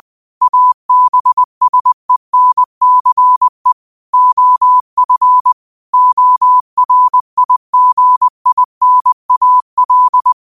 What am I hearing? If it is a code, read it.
Morse code, heard as a keyed tone, about 20 wpm, 1000 Hz. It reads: ABSENCE OF ORIGINAL
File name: 20wpm